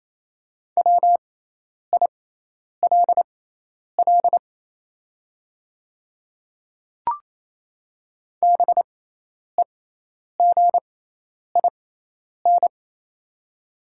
Morse code, heard as a keyed tone, about 28 wpm, spaced as for 10 wpm, 700 Hz. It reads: WILL BEGIN
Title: 6x Character Spacing - Top 300 Words - 28wpm